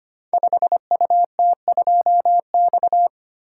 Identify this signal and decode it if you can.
Morse code, heard as a keyed tone, about 25 wpm, 700 Hz. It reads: 5UT2X
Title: Code Groups - 5 Letter:Numbers - 25wpm